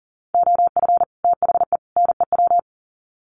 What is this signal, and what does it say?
Morse code, heard as a keyed tone, about 40 wpm, 700 Hz. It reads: OF THE NEW
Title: Sets of 3 Words 40wpm